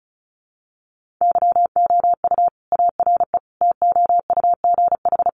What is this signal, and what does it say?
Morse code, heard as a keyed tone, about 34 wpm, 700 Hz. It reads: YOU ARE TOUGH
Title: Sentences from Top 1200 Words 35wpm